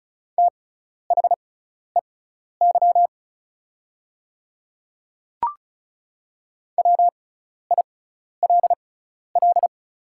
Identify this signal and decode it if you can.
Morse code, heard as a keyed tone, about 34 wpm, spaced as for 12 wpm, 700 Hz. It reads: THEY WILL
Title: 6x Character Spacing - Top 100 Words - 35wpm